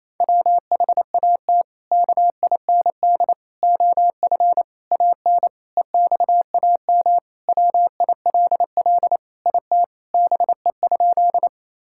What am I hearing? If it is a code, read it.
Morse code, heard as a keyed tone, about 27 wpm, 700 Hz. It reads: WHAT KIND OF AN EXAM WILL IT BE?
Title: Sentences from Top 1000 Words - Review Part 5 - 28wpm